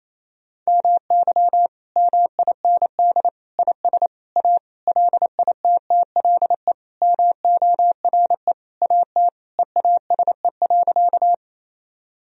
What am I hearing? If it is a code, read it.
Morse code, heard as a keyed tone, about 28 wpm, 700 Hz. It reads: MY MIND IS A LITTLE MORE AT EASE.
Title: Sentences from Top 1000 Words - Review Part 1 - 28wpm